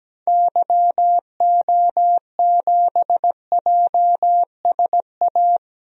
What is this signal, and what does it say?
Morse code, heard as a keyed tone, about 17 wpm, 700 Hz. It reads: YO7JSA
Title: DXCC-100 Call Signs - 6 Characters - 17wpm